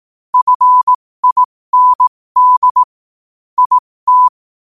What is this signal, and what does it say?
Morse code, heard as a keyed tone, about 18 wpm, spaced as for 15 wpm, 1000 Hz. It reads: FIND IT
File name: qotdfiles / 15wpm